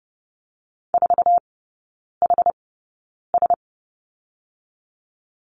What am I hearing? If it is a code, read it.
Morse code, heard as a keyed tone, about 30 wpm, spaced as for 9 wpm, 700 Hz. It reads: 4HS
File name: 081 Lesson 14e - Number 2 - Three Characters 30wpm